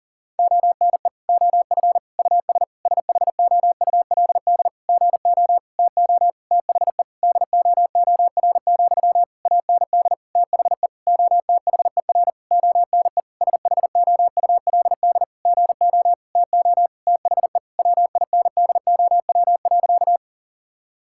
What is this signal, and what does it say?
Morse code, heard as a keyed tone, about 40 wpm, 700 Hz. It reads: ONE OF US SHOULD GO TO THE DOOR, AND THE OTHER ONE SHOULD GO TO THE WINDOW.
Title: Sentences from Top 1600 Words 40wpm